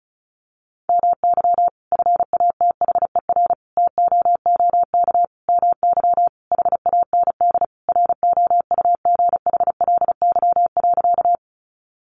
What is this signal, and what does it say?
Morse code, heard as a keyed tone, about 34 wpm, 700 Hz. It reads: MY FATHER TOOK MY HAND ROUGHLY.
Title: Sentences from Top 1500 Words 35wpm